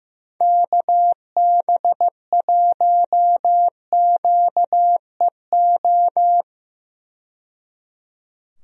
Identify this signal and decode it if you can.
Morse code, heard as a keyed tone, about 15 wpm, 700 Hz. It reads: KB1QEO
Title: USA Call Signs - 6 Characters - 15wpm